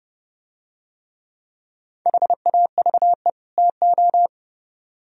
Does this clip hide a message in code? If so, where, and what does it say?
Yes: Morse code from 2.1 to 4.3 s, heard as a keyed tone, about 28 wpm, 700 Hz. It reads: HAVE TO